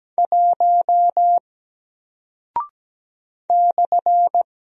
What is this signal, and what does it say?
Morse code, heard as a keyed tone, about 17 wpm, 700 Hz. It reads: 1 /
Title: Single Character (Letter:Number:Punctuation:Prosign) Alt 17wpm